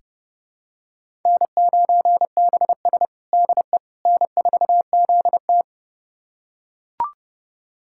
Morse code, heard as a keyed tone, about 30 wpm, 700 Hz. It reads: N9BS DE N4ZT
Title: USA Callsign DE Callsign - 4 Characters - 30wpm